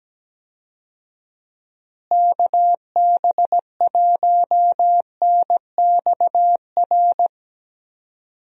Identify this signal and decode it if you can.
Morse code, heard as a keyed tone, about 17 wpm, 700 Hz. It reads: KB1NXR
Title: USA Call Signs - 6 Characters - Repeated 3x - 17wpm